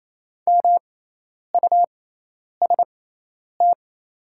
Morse code, heard as a keyed tone, about 27 wpm, spaced as for 10 wpm, 700 Hz. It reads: MUST
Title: 6x Character Spacing - Top 200 Words - 28wpm